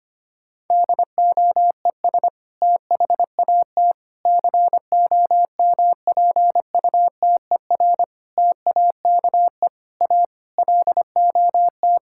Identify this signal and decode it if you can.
Morse code, heard as a keyed tone, about 25 wpm, 700 Hz. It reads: DOES THAT COMPUTER TAKE A LOT
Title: Sentences from Top 1000 Words - Review Part 6 - 25wpm